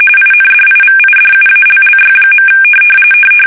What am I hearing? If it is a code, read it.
RTTY, about 45.45 baud. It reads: INDIVIDUALLY.
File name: fsk850 / benjamin-g-lamme